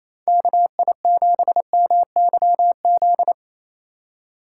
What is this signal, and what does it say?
Morse code, heard as a keyed tone, about 28 wpm, 700 Hz. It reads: KI7MYZ
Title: USA Call Signs - 3x Character Spacing - 28wpm